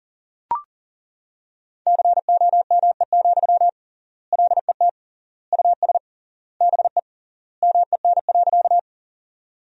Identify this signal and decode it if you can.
Morse code, heard as a keyed tone, about 40 wpm, 700 Hz. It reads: COME, LET US BE MEN.
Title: 3x Word-Spacing - Sentences from Top 300 Words - 40wpm